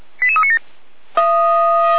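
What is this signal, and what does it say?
Five-tone selective call (ZVEI), heard as quick runs of five tones, tones of about 70 ms. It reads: 88397
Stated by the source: ZVEI five-tone call received by SDR, from feuerwehr-orlamuende.org fire brigade archive.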